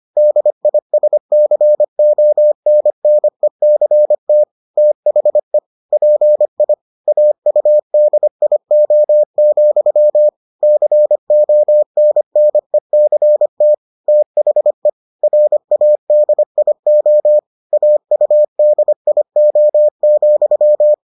Morse code, heard as a keyed tone, about 25 wpm, 600 Hz. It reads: DISCONNECT THE PI AUDIO, CONNECT THE RADIO AUDIO,